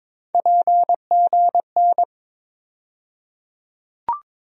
Morse code, heard as a keyed tone, about 22 wpm, 700 Hz. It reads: PGN
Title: USA Call Sign Postfixes - 3 Letters 22wpm